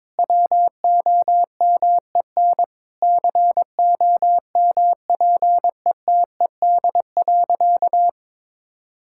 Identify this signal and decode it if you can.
Morse code, heard as a keyed tone, about 22 wpm, 700 Hz. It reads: WOMEN COMPETED.
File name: practice / 22wpm / Sentences with 3-Word Phrases - 22wpm